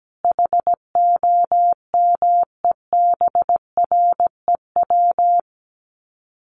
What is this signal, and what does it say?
Morse code, heard as a keyed tone, about 17 wpm, 700 Hz. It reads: HOMEBREW
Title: Common QSO Phrases 17wpm